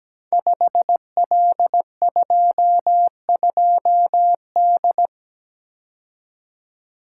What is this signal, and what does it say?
Morse code, heard as a keyed tone, about 17 wpm, 700 Hz. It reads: 5L22D
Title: Code Groups - 5 Letter:Numbers - 17wpm